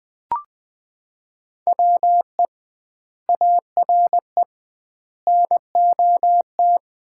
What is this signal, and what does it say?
Morse code, heard as a keyed tone, about 20 wpm, 700 Hz. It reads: WE ARE NOT
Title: 2x Word-Spacing - Sets of 5 Words - 20wpm